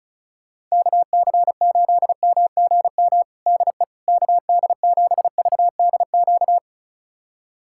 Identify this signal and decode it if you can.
Morse code, heard as a keyed tone, about 34 wpm, 700 Hz. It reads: KC8MGM DE KD7VDQ
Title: USA Callsign DE Callsign - 6 Characters - 35wpm